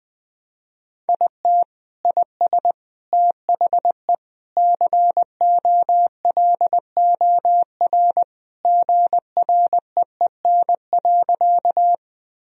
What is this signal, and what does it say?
Morse code, heard as a keyed tone, about 20 wpm, 700 Hz. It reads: IT IS THE COLOR GREEN.
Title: Sentences from Top 1000 Words - Review Part 2 - 20wpm